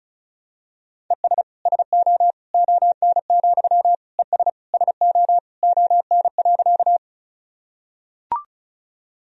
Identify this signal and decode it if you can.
Morse code, heard as a keyed tone, about 36 wpm, 700 Hz. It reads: ES SO ON, ES SO ON.
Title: Sentences from Top 100 Words and Abbreviations 35wpm